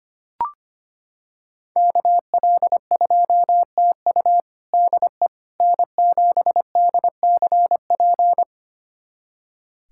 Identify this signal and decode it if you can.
Morse code, heard as a keyed tone, about 25 wpm, 700 Hz. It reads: KL2TU DE N7DCP
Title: USA Callsign DE Callsign - 5 Characters - 25wpm